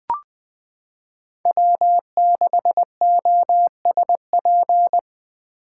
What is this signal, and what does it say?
Morse code, heard as a keyed tone, about 19 wpm, 700 Hz. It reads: W6OSP
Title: K1USN SST QSO Practice - 20wpm